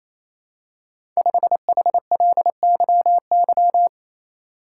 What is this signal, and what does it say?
Morse code, heard as a keyed tone, about 27 wpm, 700 Hz. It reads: 5HLYY
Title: Code Groups - 5 Letter:Numbers - 28wpm